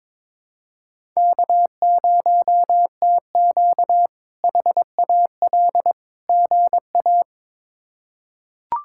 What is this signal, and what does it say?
Morse code, heard as a keyed tone, about 22 wpm, 700 Hz. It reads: K0TQ HAL GA
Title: K1USN SST QSO Practice - 22wpm